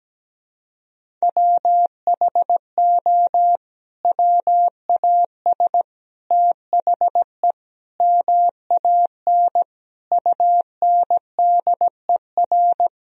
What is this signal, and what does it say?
Morse code, heard as a keyed tone, about 17 wpm, 700 Hz. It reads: WHO WAS THE MAN UNDER
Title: Sentences from Top 1000 Words - Review Part 1 - 17wpm